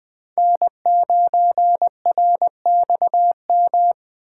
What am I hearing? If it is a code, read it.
Morse code, heard as a keyed tone, about 20 wpm, 700 Hz. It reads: N9RXM
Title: Speed Racing - USA 1x3 Call Signs 17wpm